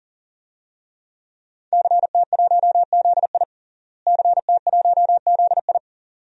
Morse code, heard as a keyed tone, about 40 wpm, 700 Hz. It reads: CT1ZI CT1ZI
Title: DXCC-100 Call Signs - 5 Characters - Repeated 2x - 40wpm